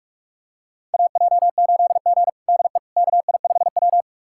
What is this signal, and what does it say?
Morse code, heard as a keyed tone, about 46 wpm, 700 Hz. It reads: AJ8G DE KI5W
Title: USA Callsign DE Callsign - 4 Characters - 45wpm